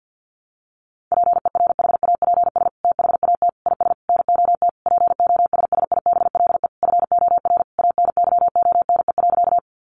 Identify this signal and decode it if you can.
Morse code, heard as a keyed tone, about 50 wpm, 700 Hz. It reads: PERHAPS THAT IS NOT POSSIBLE FOR ANYONE.
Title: Sentences from Top 900 Words 50wpm